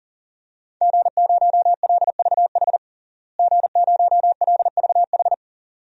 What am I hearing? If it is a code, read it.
Morse code, heard as a keyed tone, about 40 wpm, 700 Hz. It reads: G0LVH G0LVH
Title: DXCC-100 Call Signs - 5 Characters - Repeated 2x - 40wpm